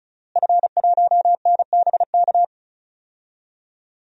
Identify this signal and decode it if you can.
Morse code, heard as a keyed tone, about 34 wpm, 700 Hz. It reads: F1NBK
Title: DXCC-100 Call Signs - 5 Characters - Repeated 2x - 35wpm